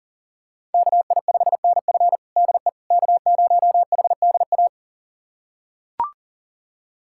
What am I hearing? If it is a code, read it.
Morse code, heard as a keyed tone, about 37 wpm, 700 Hz. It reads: KI5NF DE K0HDA
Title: USA Callsign DE Callsign - 5 Characters - 40wpm